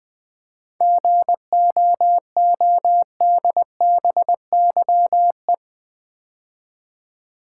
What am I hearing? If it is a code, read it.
Morse code, heard as a keyed tone, about 20 wpm, 700 Hz. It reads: GOODBYE
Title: 7 Letter Words - First 3 Letters Given - 20wpm